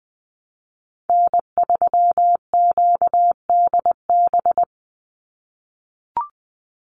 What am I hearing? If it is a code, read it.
Morse code, heard as a keyed tone, about 20 wpm, 700 Hz. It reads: N3QDB
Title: USA Call Signs 20wpm